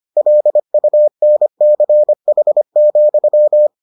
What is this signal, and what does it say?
Morse code, heard as a keyed tone, about 25 wpm, 600 Hz. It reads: LUNCH,